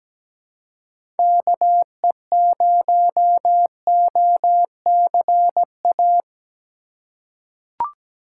Morse code, heard as a keyed tone, about 17 wpm, 700 Hz. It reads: KE0OCA
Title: USA Call Signs - 6 Characters - 6x Character Spacing - 17wpm